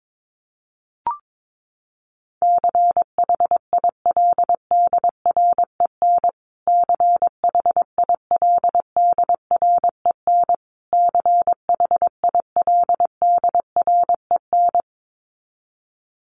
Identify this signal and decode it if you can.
Morse code, heard as a keyed tone, about 22 wpm, 700 Hz. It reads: CHILDREN CHILDREN CHILDREN
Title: Top 500 Words Repeated 22wpm